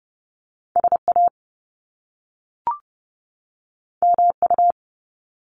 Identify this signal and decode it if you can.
Morse code, heard as a keyed tone, about 30 wpm, 700 Hz. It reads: SA MU
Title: Top Two Letter Combinations in Words 30wpm